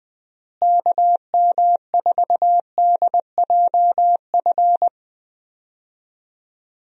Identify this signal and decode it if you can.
Morse code, heard as a keyed tone, about 21 wpm, 700 Hz. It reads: KM4DJF
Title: USA Call Signs - 6 Characters - Repeated 2x - 20wpm